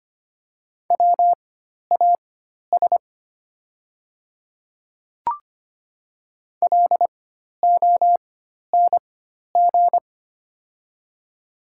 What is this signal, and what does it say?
Morse code, heard as a keyed tone, about 25 wpm, spaced as for 12 wpm, 700 Hz. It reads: WAS LONG
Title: 4x Character Spacing - Top 100 Words - 25wpm